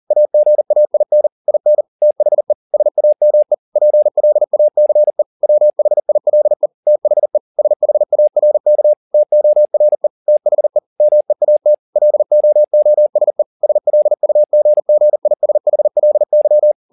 Morse code, heard as a keyed tone, about 40 wpm, 600 Hz. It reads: AGAIN IN THE SAME PLACE WHILE THE SHARK TORE THE MEAT LOOSE SLUGGISHLY
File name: the old man and the sea 2 40wpm